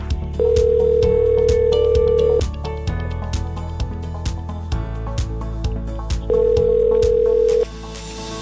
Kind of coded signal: telephone call progress tone